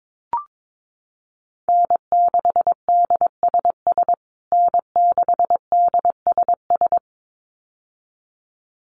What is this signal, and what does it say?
Morse code, heard as a keyed tone, about 22 wpm, 700 Hz. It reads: N6DSS N6DSS
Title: USA Call Signs Repeated 2x 22wpm